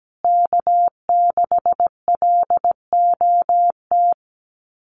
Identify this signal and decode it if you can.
Morse code, heard as a keyed tone, about 17 wpm, 700 Hz. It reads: K6LOT
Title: USA Call Signs Repeated 2x 17wpm